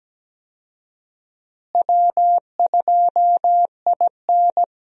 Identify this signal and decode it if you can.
Morse code, heard as a keyed tone, about 17 wpm, 700 Hz. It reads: W2IN
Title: USA Call Signs - 4 Characters - Repeated 3x - 17wpm